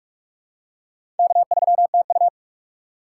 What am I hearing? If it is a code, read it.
Morse code, heard as a keyed tone, about 45 wpm, 700 Hz. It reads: K3TU
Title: USA Call Signs - 4 Characters - 4x Character Spacing - 45wpm